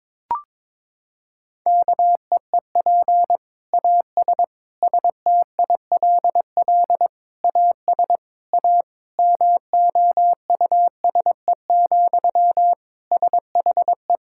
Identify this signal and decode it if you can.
Morse code, heard as a keyed tone, about 22 wpm, 700 Hz. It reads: KEEP AS STILL AS A MOUSE, SHE
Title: Sentences from Top 1000 Words - Review Part 5 - 22wpm